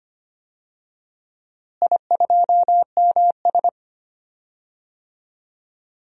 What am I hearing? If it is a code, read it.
Morse code, heard as a keyed tone, about 25 wpm, 700 Hz. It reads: I2MS
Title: DXCC-100 Call Signs - 4 Characters - Repeated 2x - 25wpm